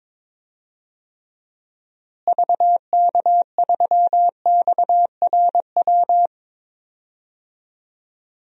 Morse code, heard as a keyed tone, about 22 wpm, 700 Hz. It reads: VK3XRW
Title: DXCC-100 Call Signs - 6 Characters - Repeated 2x - 22wpm